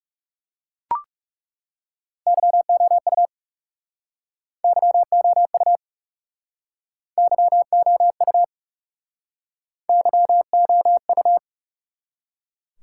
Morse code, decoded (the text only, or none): YOU YOU YOU YOU